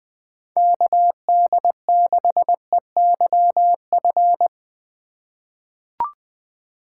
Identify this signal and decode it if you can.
Morse code, heard as a keyed tone, about 20 wpm, 700 Hz. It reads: KD6EYF
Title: USA Call Signs - 6 Characters - 4x Character Spacing - 20wpm